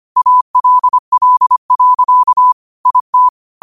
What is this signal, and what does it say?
Morse code, heard as a keyed tone, about 25 wpm, 1000 Hz. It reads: ALL. IT
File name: qotdfiles / 25wpm